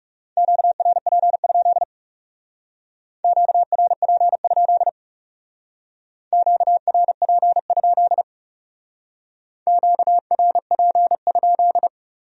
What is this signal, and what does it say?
Morse code, heard as a keyed tone, about 39 wpm, 700 Hz. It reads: QRP? QRP? QRP? QRP?